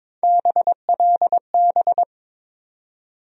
Morse code, heard as a keyed tone, about 22 wpm, 700 Hz. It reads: BLB
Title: USA Call Sign Postfixes - 3 Letters 22wpm